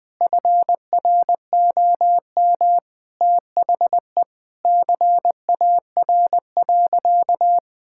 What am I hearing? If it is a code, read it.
Morse code, heard as a keyed tone, about 20 wpm, 700 Hz. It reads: FROM THE CAR.
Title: Sentences from Top 1000 Words - Review Part 5 - 20wpm